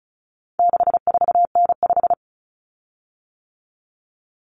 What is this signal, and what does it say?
Morse code, heard as a keyed tone, about 34 wpm, 700 Hz. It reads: 64N5
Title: Common Cut Numbers - 4-Digit 35wpm